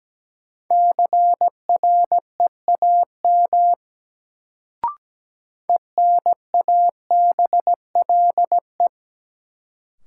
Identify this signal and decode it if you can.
Morse code, heard as a keyed tone, about 17 wpm, 700 Hz. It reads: CREAM ENABLE
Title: Sentences from Top 1500 Words - New Word Drill - 17wpm